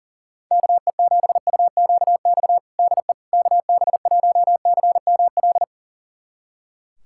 Morse code, heard as a keyed tone, about 40 wpm, 700 Hz. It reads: KE7UQX DE KB1CML
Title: USA Callsign DE Callsign 40wpm